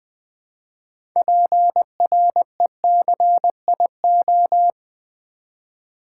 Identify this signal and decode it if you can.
Morse code, heard as a keyed tone, about 20 wpm, 700 Hz. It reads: PRECIO
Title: Build A Word - 3 Letters At A Time - 8-Letter Words - 20wpm